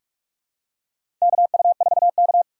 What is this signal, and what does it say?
Morse code, heard as a keyed tone, about 44 wpm, 700 Hz. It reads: KU4K
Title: USA Call Signs - 4 Characters - 6x Character Spacing - 45wpm